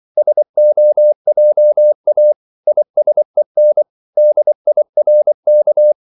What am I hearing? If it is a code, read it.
Morse code, heard as a keyed tone, about 24 wpm, 600 Hz. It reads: SOJA ISEN DIRK
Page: Korte danske ord 24wpm